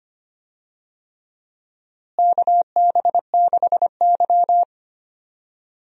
Morse code, heard as a keyed tone, about 25 wpm, 700 Hz. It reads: KB6Y
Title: USA Call Signs - 4 Characters - Repeated 3x - 25wpm